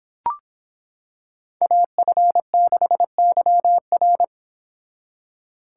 Morse code, heard as a keyed tone, about 26 wpm, 700 Hz. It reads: AF6YR